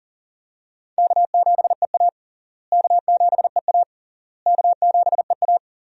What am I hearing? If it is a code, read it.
Morse code, heard as a keyed tone, about 40 wpm, 700 Hz. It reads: K7EA K7EA K7EA
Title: USA Call Signs - 4 Characters - Repeated 3x - 40wpm